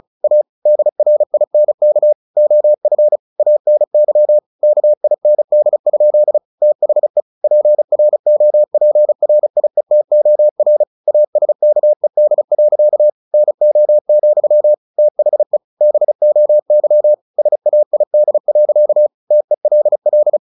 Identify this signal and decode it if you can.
Morse code, heard as a keyed tone, about 34 wpm, 600 Hz. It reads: A DRINK OF ANY KIND? THE PROPRIETOR ASKED. NO, THE BOY SAID. TELL
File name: the old man and the sea 2 35wpm